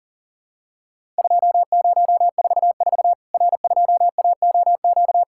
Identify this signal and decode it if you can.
Morse code, heard as a keyed tone, about 40 wpm, 700 Hz. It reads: 2044 R2AOQ
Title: Worked All Europe DX Contest - QTC Exchange - 40wpm